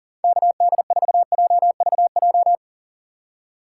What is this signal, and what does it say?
Morse code, heard as a keyed tone, about 46 wpm, 700 Hz. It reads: KD4JVJ
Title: Speed Racing - USA 2x3 Call Signs 40wpm